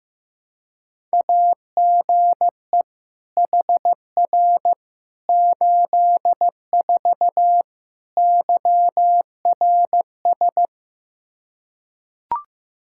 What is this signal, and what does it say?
Morse code, heard as a keyed tone, about 16 wpm, 700 Hz. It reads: AGE HR 84 YRS
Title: Practice QSO - Exchange 3 - More Info - 15wpm